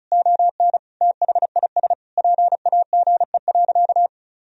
Morse code, heard as a keyed tone, about 36 wpm, 700 Hz. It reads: ON THIS PAGE.
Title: Sentences from Top 1000 Words - Review Part 1 - 35wpm